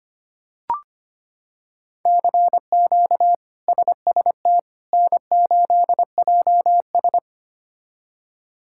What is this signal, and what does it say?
Morse code, heard as a keyed tone, about 25 wpm, 700 Hz. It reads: CQ SST N8JS
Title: K1USN SST QSO Practice - 25wpm